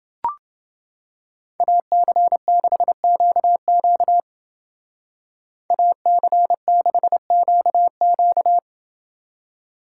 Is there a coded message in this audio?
Yes: Morse code from 1.6 to 8.6 s, heard as a keyed tone, about 28 wpm, 700 Hz. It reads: AC6QQ AC6QQ